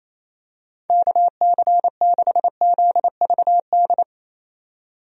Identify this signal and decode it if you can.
Morse code, heard as a keyed tone, about 27 wpm, 700 Hz. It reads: KC6ZVD
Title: USA Call Signs - 6 Characters - 28wpm